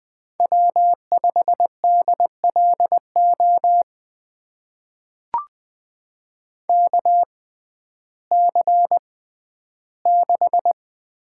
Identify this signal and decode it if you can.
Morse code, heard as a keyed tone, about 20 wpm, spaced as for 12 wpm, 700 Hz. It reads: W5DLO KC6
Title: USA Call Signs - 6x Character Spacing - 20wpm